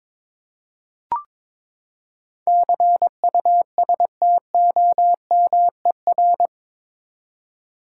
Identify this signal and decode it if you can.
Morse code, heard as a keyed tone, about 22 wpm, 700 Hz. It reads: CUSTOMER
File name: practice / 22wpm / Eight Letter Words - 22wpm